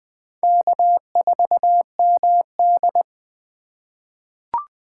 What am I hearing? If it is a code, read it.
Morse code, heard as a keyed tone, about 20 wpm, 700 Hz. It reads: K4MD
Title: USA Call Signs - 2x Character Spacing - 20wpm